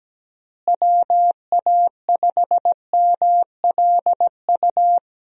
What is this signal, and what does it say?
Morse code, heard as a keyed tone, about 17 wpm, 700 Hz. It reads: WA5MLU
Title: USA Call Signs - 6 Characters - Repeated 3x - 17wpm